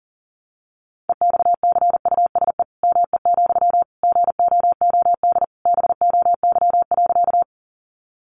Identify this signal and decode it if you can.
Morse code, heard as a keyed tone, about 40 wpm, 700 Hz. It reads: EXCUSE ME, GOOD BOY.
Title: Sentences from Top 700 Words 40wpm